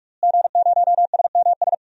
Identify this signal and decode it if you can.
Morse code, heard as a keyed tone, about 46 wpm, 700 Hz. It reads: G0SMS
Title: DXCC-100 Call Signs - 5 Characters - Repeated 2x - 45wpm